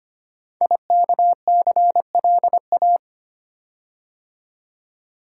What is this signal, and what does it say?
Morse code, heard as a keyed tone, about 25 wpm, 700 Hz. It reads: IKCLA
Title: Code Groups - 5 Letter:Numbers - 25wpm